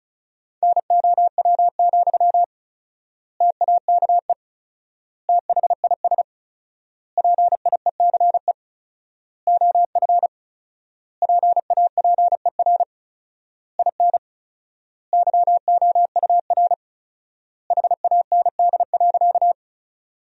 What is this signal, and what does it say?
Morse code, heard as a keyed tone, about 36 wpm, 700 Hz. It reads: NOW, TAKE THIS PIECE OF PAPER IN YOUR HAND.
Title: 4x Word-Spacing - Sentences from Top 400 Words - 35wpm